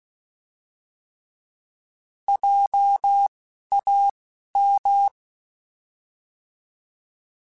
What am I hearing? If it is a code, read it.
Morse code, heard as a keyed tone, about 16 wpm, spaced as for 11 wpm, 800 Hz. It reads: JAM
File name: Alphabetical / Alphabetical mp3s / Js @10